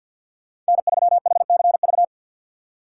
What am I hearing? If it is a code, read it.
Morse code, heard as a keyed tone, about 53 wpm, 700 Hz. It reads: N3HCV
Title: USA Call Signs - 5 Characters - 2x Character Spacing - 50wpm